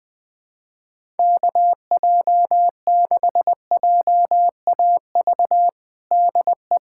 Morse code, heard as a keyed tone, about 20 wpm, 700 Hz. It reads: KJ6JAV DE
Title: USA Callsign DE Callsign 20wpm